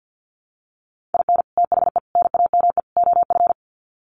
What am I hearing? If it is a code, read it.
Morse code, heard as a keyed tone, about 50 wpm, 700 Hz. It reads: IN THE NAME OF
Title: Sets of 4 Words 50wpm